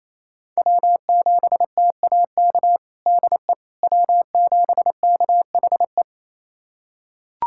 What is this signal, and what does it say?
Morse code, heard as a keyed tone, about 28 wpm, 700 Hz. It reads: W7TAK DE W7KHE
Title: USA Callsign DE Callsign - 5 Characters - 28wpm